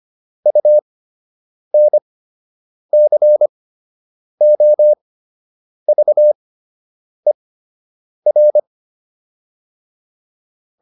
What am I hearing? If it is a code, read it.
Morse code, heard as a keyed tone, about 25 wpm, spaced as for 8 wpm, 600 Hz. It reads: UNCOVER